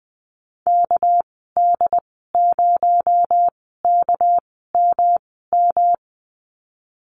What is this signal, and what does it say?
Morse code, heard as a keyed tone, about 20 wpm, spaced as for 15 wpm, 700 Hz. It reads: KD0KMM
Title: 219 Lesson 35g - Number 0 - Call Signs 20wpm